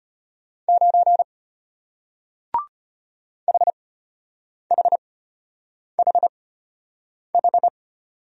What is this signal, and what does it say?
Morse code, heard as a keyed tone, about 33 wpm, spaced as for 8 wpm, 700 Hz. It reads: 9 HHHH